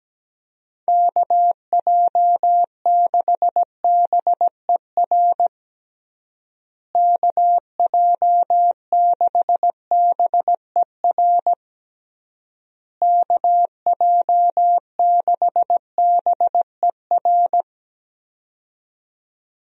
Morse code, heard as a keyed tone, about 17 wpm, 700 Hz. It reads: KJ6BER KJ6BER KJ6BER
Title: USA Call Signs - 6 Characters - Repeated 3x - 17wpm